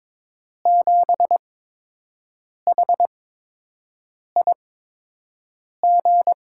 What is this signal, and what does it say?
Morse code, heard as a keyed tone, about 22 wpm, spaced as for 6 wpm, 700 Hz. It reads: 7HIG
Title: USA Call Signs - 8x Character Spacing - 22wpm